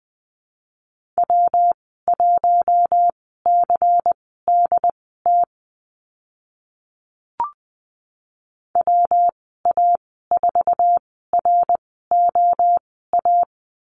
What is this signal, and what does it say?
Morse code, heard as a keyed tone, about 20 wpm, spaced as for 14 wpm, 700 Hz. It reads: W1CDT WA4ROA
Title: 161 Lesson 26g - Letter g - Call Signs 20wpm